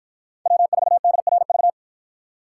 Morse code, heard as a keyed tone, about 53 wpm, 700 Hz. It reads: W4DRV